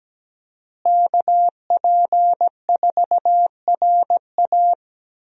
Morse code, heard as a keyed tone, about 17 wpm, 700 Hz. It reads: KP4RA
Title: DXCC-100 Call Signs - 5 Characters - Repeated 3x - 17wpm